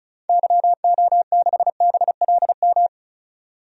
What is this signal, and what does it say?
Morse code, heard as a keyed tone, about 36 wpm, 700 Hz. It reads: YO6BLM
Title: DXCC-100 Call Signs - 6 Characters - 35wpm